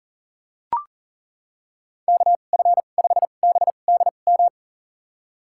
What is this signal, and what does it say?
Morse code, heard as a keyed tone, about 37 wpm, spaced as for 28 wpm, 700 Hz. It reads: KF5BDM
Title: USA Call Signs - 6 Characters - 2x Character Spacing - 40wpm